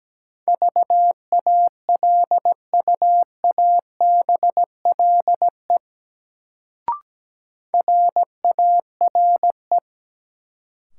Morse code, heard as a keyed tone, about 17 wpm, 700 Hz. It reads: VALUABLE RARE
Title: Sentences from Top 1100 Words - New Word Drill - 17wpm